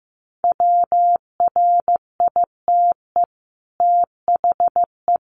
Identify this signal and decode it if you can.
Morse code, heard as a keyed tone, about 15 wpm, 700 Hz. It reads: WRITE THE
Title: Sentences from Top 1400 Words 15wpm